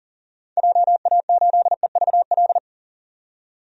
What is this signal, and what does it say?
Morse code, heard as a keyed tone, about 40 wpm, 700 Hz. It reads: JA8EVL
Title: DXCC-100 Call Signs - 6 Characters - 40wpm